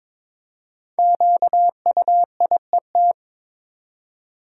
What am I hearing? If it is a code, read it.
Morse code, heard as a keyed tone, about 22 wpm, 700 Hz. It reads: QUIET
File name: 5 Letter Words - First 3 Letters Given - 22wpm